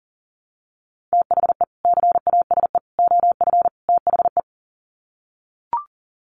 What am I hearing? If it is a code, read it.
Morse code, heard as a keyed tone, about 40 wpm, 700 Hz. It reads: THE CASE OF THE
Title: Sets of 4 Words 40wpm